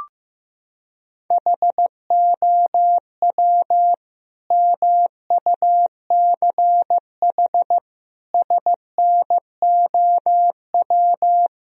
Morse code, heard as a keyed tone, about 15 wpm, 700 Hz. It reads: HOW MUCH SNOW
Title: Sentences from Top 1000 Words - Review Part 3 - 15wpm